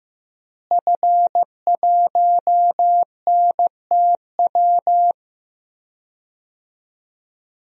Morse code, heard as a keyed tone, about 15 wpm, 700 Hz. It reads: F1NTW
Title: DXCC-100 Call Signs - 5 Characters - Repeated 2x - 15wpm